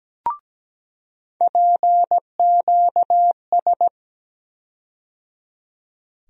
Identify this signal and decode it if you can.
Morse code, heard as a keyed tone, about 17 wpm, 700 Hz. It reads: PQS
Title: USA Call Sign Postfixes - 3 Letters 17wpm